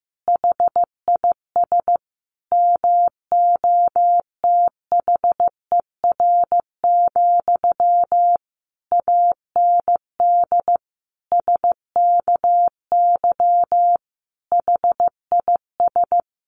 Morse code, heard as a keyed tone, about 14 wpm, 700 Hz. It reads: HIS MOTHER, AND SKY HIS
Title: Sentences from Top 1200 Words 15wpm